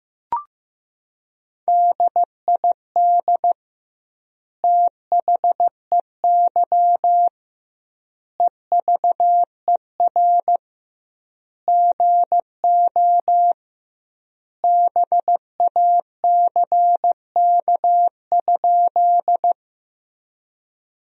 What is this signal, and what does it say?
Morse code, heard as a keyed tone, about 15 wpm, 700 Hz. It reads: DID THEY EVER GO BACK?
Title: 2x Word-Spacing - Sentences from Top 400 Words - 15wpm